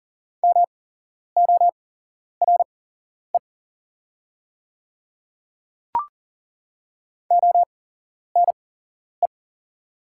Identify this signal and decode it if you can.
Morse code, heard as a keyed tone, about 40 wpm, spaced as for 11 wpm, 700 Hz. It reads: MORE ONE
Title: 8x Character Spacing - Top 200 Words - 40wpm